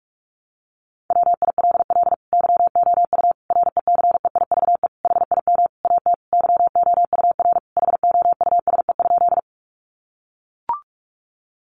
Morse code, heard as a keyed tone, about 44 wpm, 700 Hz. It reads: WILL YOU RECEIVE HIM AT YOUR HOUSE?
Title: Sentences from Top 1000 Words 45wpm